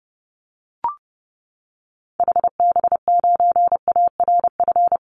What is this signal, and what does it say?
Morse code, heard as a keyed tone, about 30 wpm, 700 Hz. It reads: HB9ARF
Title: Call Signs - 30wpm